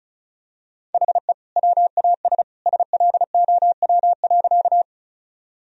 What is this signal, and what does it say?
Morse code, heard as a keyed tone, about 34 wpm, 700 Hz. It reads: HE WAS SLOW.
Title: Sentences from Top 1000 Words - Review Part 2 - 35wpm